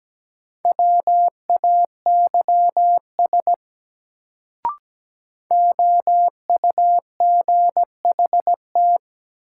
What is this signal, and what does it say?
Morse code, heard as a keyed tone, about 17 wpm, 700 Hz. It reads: WAYS OUGHT
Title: Sets of 5 Words Encore - New Word Drill - 17wpm